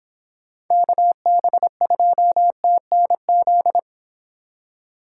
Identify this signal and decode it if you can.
Morse code, heard as a keyed tone, about 26 wpm, 700 Hz. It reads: KB2TNZ